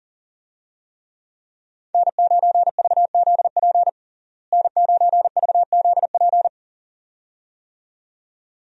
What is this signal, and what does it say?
Morse code, heard as a keyed tone, about 40 wpm, 700 Hz. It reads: N9VZP N9VZP
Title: USA Call Signs - 5 Characters - Repeated 2x - 40wpm